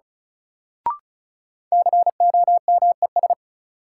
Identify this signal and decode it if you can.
Morse code, heard as a keyed tone, about 36 wpm, 700 Hz. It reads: COMES
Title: Sentences from Top 1000 Words - Review Part 5 - New Word Drill - 35wpm